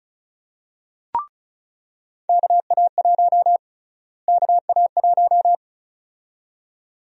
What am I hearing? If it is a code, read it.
Morse code, heard as a keyed tone, about 34 wpm, 700 Hz. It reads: KA1 KA1
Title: USA Call Sign Prefixes - Repeated 2x - 35wpm